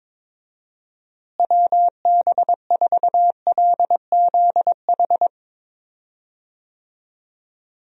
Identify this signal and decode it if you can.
Morse code, heard as a keyed tone, about 22 wpm, 700 Hz. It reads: WB4LZH
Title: USA Call Signs - 6 Characters - Repeated 3x - 22wpm